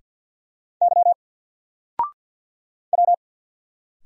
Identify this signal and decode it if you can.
Morse code, heard as a keyed tone, about 50 wpm, 700 Hz. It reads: Y W
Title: Single Letter-Number Alt 50wpm